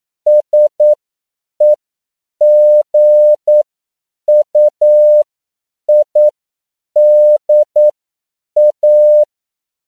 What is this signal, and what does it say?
Morse code, heard as a keyed tone, about 9 wpm, spaced as for 7 wpm, 600 Hz. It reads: SEGUIDA